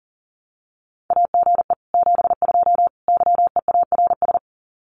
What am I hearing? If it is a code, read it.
Morse code, heard as a keyed tone, about 40 wpm, 700 Hz. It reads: AGE 72 YEARS
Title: Common QSO Phrases Encore 40wpm